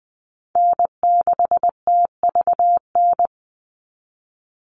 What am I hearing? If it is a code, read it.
Morse code, heard as a keyed tone, about 20 wpm, 700 Hz. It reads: N6TVN
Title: Call Signs Repeated 2x - 20wpm